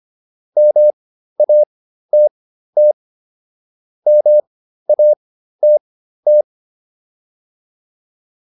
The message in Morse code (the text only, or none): MATT MATT